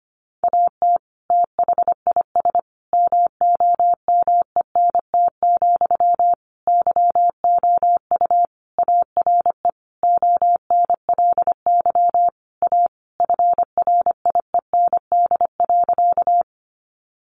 Morse code, heard as a keyed tone, about 25 wpm, 700 Hz. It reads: AT THIS MOMENT, YOU ARE ONLY A FRIEND.
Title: Sentences from Top 1000 Words 25wpm